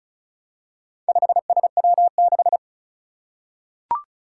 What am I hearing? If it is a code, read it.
Morse code, heard as a keyed tone, about 36 wpm, 700 Hz. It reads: 5SW6
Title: Four Challenging Letter-Number Combinations 35wpm